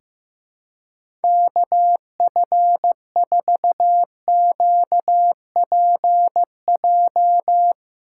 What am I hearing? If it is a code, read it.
Morse code, heard as a keyed tone, about 15 wpm, 700 Hz. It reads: KF4QPJ
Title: USA Call Signs - 6 Characters - 2x Character Spacing - 15wpm